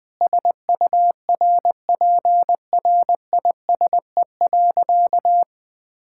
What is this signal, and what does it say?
Morse code, heard as a keyed tone, about 19 wpm, 700 Hz. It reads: SURPRISE.
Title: Sentences with 5-Word Phrases - Spoken Beginning - 20wpm